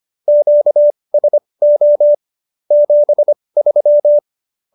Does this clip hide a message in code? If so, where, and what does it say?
Yes: Morse code from 0.3 to 4.2 s, heard as a keyed tone, about 25 wpm, spaced as for 20 wpm, 590 Hz. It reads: QSO 73